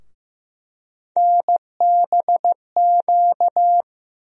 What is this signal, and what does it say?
Morse code, heard as a keyed tone, about 15 wpm, 700 Hz. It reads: NBQ
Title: USA Call Sign Postfixes - 3 Letters 15wpm